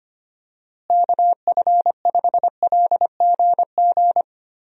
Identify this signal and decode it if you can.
Morse code, heard as a keyed tone, about 25 wpm, 700 Hz. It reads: KF5LGG
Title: USA Call Signs - 6 Characters - Repeated 2x - 25wpm